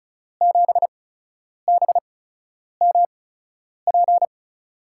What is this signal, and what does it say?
Morse code, heard as a keyed tone, about 36 wpm, spaced as for 10 wpm, 700 Hz. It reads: 7BMP
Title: USA Call Signs - 6 Characters - 8x Character Spacing - 35wpm